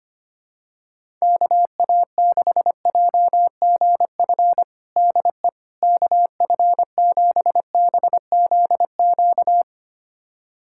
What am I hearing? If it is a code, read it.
Morse code, heard as a keyed tone, about 25 wpm, 700 Hz. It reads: KA6JGF DE KF7BZQ
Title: USA Callsign DE Callsign - 6 Characters - 25wpm